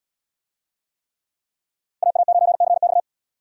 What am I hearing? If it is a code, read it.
Morse code, heard as a keyed tone, about 77 wpm, 690 Hz. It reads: AA9CX